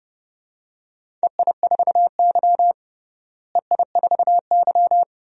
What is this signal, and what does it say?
Morse code, heard as a keyed tone, about 30 wpm, 700 Hz. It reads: EI4Y EI4Y
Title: DXCC-100 Call Signs - 4 Characters - Repeated 2x - 30wpm